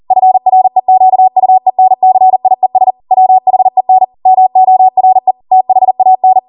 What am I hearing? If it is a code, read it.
Morse code, heard as a keyed tone, about 40 wpm, 750 Hz. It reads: FREQUENCIES WHEN MORE THAN
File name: Morse / 40WPM